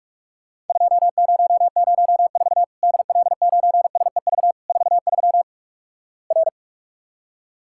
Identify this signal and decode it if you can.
Morse code, heard as a keyed tone, about 48 wpm, 700 Hz. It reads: 2004 DL9SEV 43 R
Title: Worked All Europe DX Contest - QTC Exchange - 45wpm